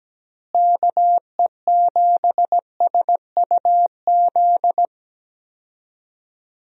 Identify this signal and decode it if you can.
Morse code, heard as a keyed tone, about 17 wpm, 700 Hz. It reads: KE7SUZ
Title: USA Call Signs - 6 Characters - Repeated 2x - 17wpm